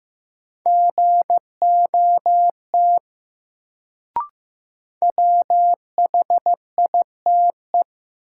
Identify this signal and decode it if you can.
Morse code, heard as a keyed tone, about 15 wpm, 700 Hz. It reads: GOT WHITE
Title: Sets of 2 Words - New Word Drill - 15wpm